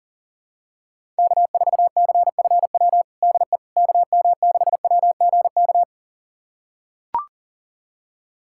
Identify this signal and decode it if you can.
Morse code, heard as a keyed tone, about 40 wpm, 700 Hz. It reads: K4CFW DE KM6WGK
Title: USA Callsign DE Callsign 40wpm